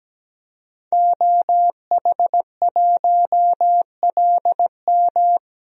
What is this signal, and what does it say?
Morse code, heard as a keyed tone, about 17 wpm, 700 Hz. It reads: OH1LM
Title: DXCC-100 Call Signs - 5 Characters - Repeated 2x - 17wpm